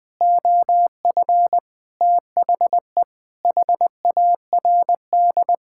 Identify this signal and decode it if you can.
Morse code, heard as a keyed tone, about 19 wpm, 700 Hz. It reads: OF THE HARD
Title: Sentences with 4-Word Phrases - 20wpm